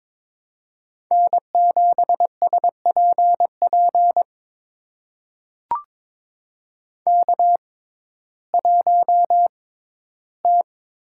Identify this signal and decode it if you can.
Morse code, heard as a keyed tone, about 22 wpm, spaced as for 13 wpm, 700 Hz. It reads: N7SPP K1T
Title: USA Call Signs - 5 Characters - 6x Character Spacing - 22wpm